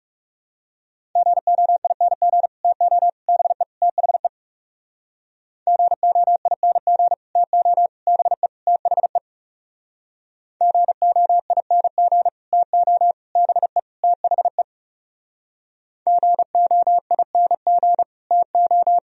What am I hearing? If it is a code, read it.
Morse code, heard as a keyed tone, about 37 wpm, 700 Hz. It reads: GOING TO BE THE GOING TO BE THE GOING TO BE THE GOING TO
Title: Speed-Racing - Sets of 4 Words 30wpm